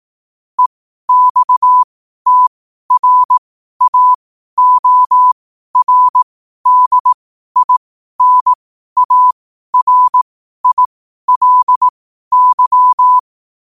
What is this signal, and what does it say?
Morse code, heard as a keyed tone, about 18 wpm, spaced as for 12 wpm, 1000 Hz. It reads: EXTRAORDINARILY
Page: New quotes every day in morse code at 12 Words per minute.